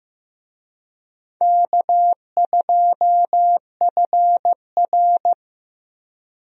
Morse code, heard as a keyed tone, about 15 wpm, 700 Hz. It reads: K2FR
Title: USA Call Signs - 4 Characters - Repeated 2x - 15wpm